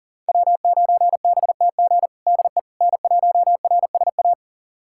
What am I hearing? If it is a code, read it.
Morse code, heard as a keyed tone, about 40 wpm, 690 Hz. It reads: W9BTG DE N1RSA
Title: USA Callsign DE Callsign - 5 Characters - 40wpm